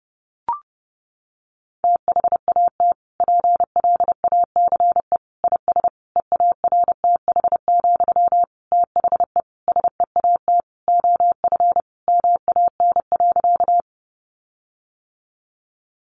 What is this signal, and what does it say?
Morse code, heard as a keyed tone, about 30 wpm, 700 Hz. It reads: THAT PLACE IS EARTH, THE SEAT OF MAN.
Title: Sentences from Top 1500 Words 30wpm